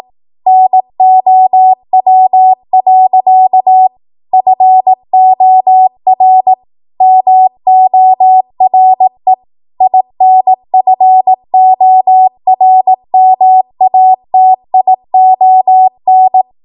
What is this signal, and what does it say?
Morse code, heard as a keyed tone, about 18 wpm, 750 Hz. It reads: NOW. FOR MORE INFORMATION
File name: Archive / 18 WPM / 18WPM